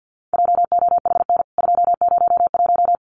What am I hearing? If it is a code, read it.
Morse code, heard as a keyed tone, about 48 wpm, 700 Hz. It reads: JOHN 201
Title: CWT Exchange 50wpm